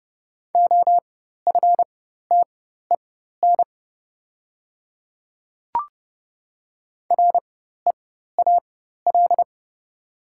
Morse code, heard as a keyed tone, about 30 wpm, spaced as for 14 wpm, 700 Hz. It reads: OFTEN REAL
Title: 4x Character Spacing - Top 300 Words - 30wpm